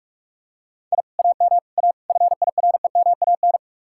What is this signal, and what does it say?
Morse code, heard as a keyed tone, about 42 wpm, 700 Hz. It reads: I AM A FIREMAN